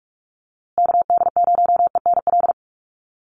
Morse code, heard as a keyed tone, about 44 wpm, 690 Hz. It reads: KD0ENL
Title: USA Call Signs Repeated 3x 45wpm